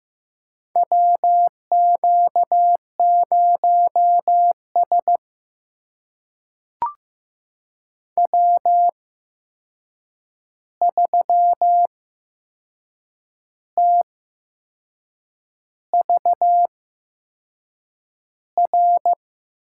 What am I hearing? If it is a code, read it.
Morse code, heard as a keyed tone, about 15 wpm, spaced as for 6 wpm, 700 Hz. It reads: WQ0S W3TVR
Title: USA Call Signs - 8x Character Spacing - 15wpm